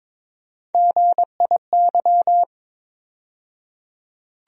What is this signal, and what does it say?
Morse code, heard as a keyed tone, about 22 wpm, 700 Hz. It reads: GIY
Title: USA Call Sign Postfixes - 3 Letters 22wpm